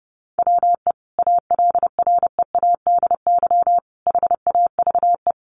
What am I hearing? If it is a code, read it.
Morse code, heard as a keyed tone, about 28 wpm, 700 Hz. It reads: WE ALREADY HAVE
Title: Sentences from Top 700 Words 30wpm